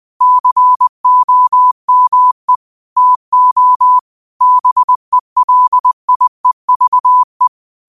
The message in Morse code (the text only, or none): COME TO BELIEVE